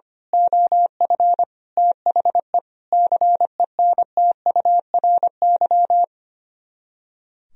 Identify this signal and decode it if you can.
Morse code, heard as a keyed tone, about 24 wpm, 700 Hz. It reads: OF THE CENTURY
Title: Speed-Racing - Sets of 5 Words 25wpm